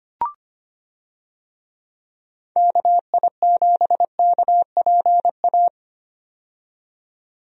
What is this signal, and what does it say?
Morse code, heard as a keyed tone, about 25 wpm, 700 Hz. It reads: KI7KPA
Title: USA Call Signs - 6 Characters - 25wpm